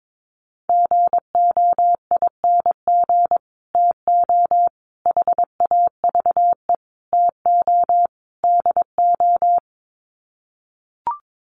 Morse code, heard as a keyed tone, about 22 wpm, 700 Hz. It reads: GOING TO HAVE TO DO
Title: Sets of 5 Words Encore 22wpm